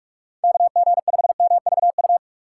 Morse code, heard as a keyed tone, about 46 wpm, 700 Hz. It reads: KG5MVU
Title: USA Call Signs - 6 Characters - 4x Character Spacing - 45wpm